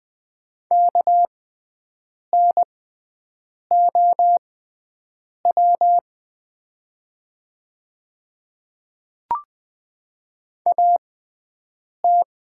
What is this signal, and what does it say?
Morse code, heard as a keyed tone, about 20 wpm, spaced as for 7 wpm, 700 Hz. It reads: KNOW AT
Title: 6x Character Spacing - Top 100 Words - 20wpm